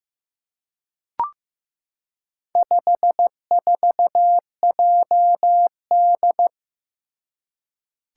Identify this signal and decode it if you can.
Morse code, heard as a keyed tone, about 15 wpm, 700 Hz. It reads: 54JD
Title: Four Challenging Letter-Number Combinations 15wpm